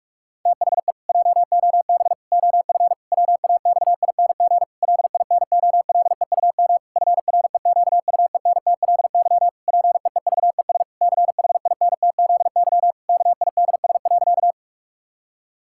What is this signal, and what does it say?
Morse code, heard as a keyed tone, about 46 wpm, 700 Hz. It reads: THE JOB OF WAXING LINOLEUM FREQUENTLY PEEVES CHINTZY KIDS.
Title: Pangram Challenge v2 - 45wpm